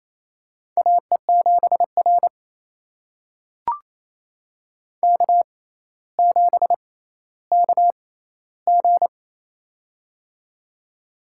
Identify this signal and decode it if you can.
Morse code, heard as a keyed tone, about 27 wpm, spaced as for 14 wpm, 700 Hz. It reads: AE7R K7KG
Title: USA Call Signs - 4 Characters - 6x Character Spacing - 28wpm